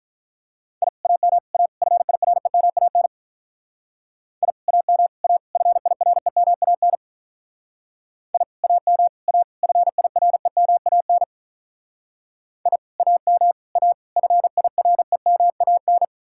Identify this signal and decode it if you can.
Morse code, heard as a keyed tone, about 44 wpm, 700 Hz. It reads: I AM A FIREMAN I AM A FIREMAN I AM A FIREMAN I AM A FIREMAN